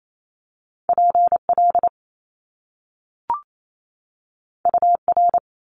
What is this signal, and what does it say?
Morse code, heard as a keyed tone, about 28 wpm, 700 Hz. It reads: PL UR
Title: Top Two Letter Combinations in Words 28wpm